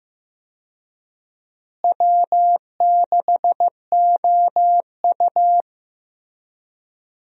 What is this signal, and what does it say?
Morse code, heard as a keyed tone, about 15 wpm, 700 Hz. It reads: W6OU
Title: USA Call Signs - 4 Characters - Repeated 3x - 15wpm